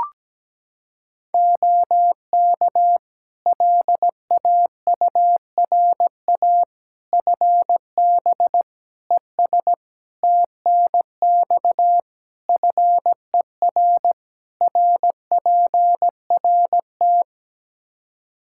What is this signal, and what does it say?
Morse code, heard as a keyed tone, about 17 wpm, 700 Hz. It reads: OK LAURA FB ES TNX FER RPRT
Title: K1USN SST QSO Practice 17wpm